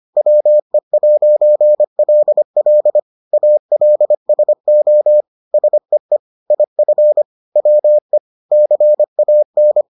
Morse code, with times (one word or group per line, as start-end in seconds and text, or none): 0.2-3.0 WE'LL
3.3-5.2 ALSO
5.5-6.2 SEE
6.5-7.2 IF
7.6-8.2 WE
8.5-9.8 CAN